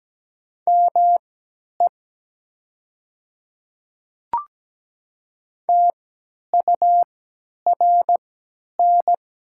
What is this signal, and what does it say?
Morse code, heard as a keyed tone, about 17 wpm, spaced as for 10 wpm, 700 Hz. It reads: ME TURN
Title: 3x Character Spacing - Top 200 Words - 17wpm